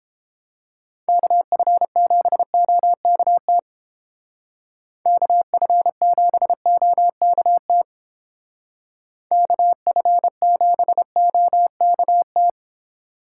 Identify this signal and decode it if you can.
Morse code, heard as a keyed tone, about 28 wpm, 700 Hz. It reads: KF7OKT KF7OKT KF7OKT